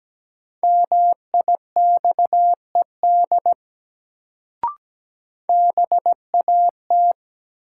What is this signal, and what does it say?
Morse code, heard as a keyed tone, about 17 wpm, 700 Hz. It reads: MIXED BAT
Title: Sentences from Top 1500 Words - New Word Drill Alt - 17wpm